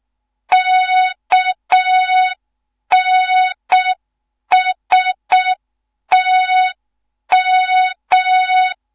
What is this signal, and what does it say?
Morse code, heard as a keyed tone, about 6 wpm, 3700 Hz. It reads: KNSTM
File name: Morze-HA2MN / hang / 15 GYAKORLAT